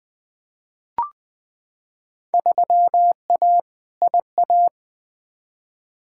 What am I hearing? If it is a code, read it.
Morse code, heard as a keyed tone, about 20 wpm, 700 Hz. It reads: 3A IA
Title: Field Day Exchanges Encore - 20wpm